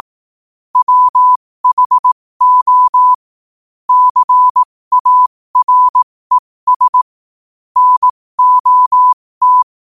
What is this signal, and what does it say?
Morse code, heard as a keyed tone, about 18 wpm, spaced as for 15 wpm, 1000 Hz. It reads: WHO CARES NOT
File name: qotdfiles / 15wpm